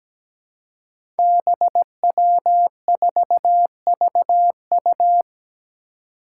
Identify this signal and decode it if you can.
Morse code, heard as a keyed tone, about 17 wpm, 700 Hz. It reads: BW4VU
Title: Five Challenging Letter-Number Combinations 17wpm